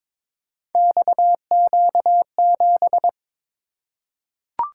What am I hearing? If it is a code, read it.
Morse code, heard as a keyed tone, about 22 wpm, 700 Hz. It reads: XQ7
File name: practice / 22wpm / Three Letter-Number Combinations - 22wpm